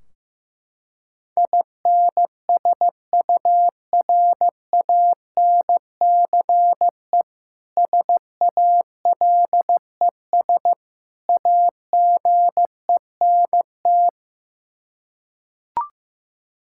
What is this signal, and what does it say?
Morse code, heard as a keyed tone, about 15 wpm, 700 Hz. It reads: INSURANCE SALES AGENT
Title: Practice QSO - Exchange 3 - More Info - 15wpm